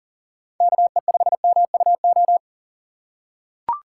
Morse code, heard as a keyed tone, about 40 wpm, 700 Hz. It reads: KE5MUO
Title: USA Call Signs - 6 Characters - 2x Character Spacing - 40wpm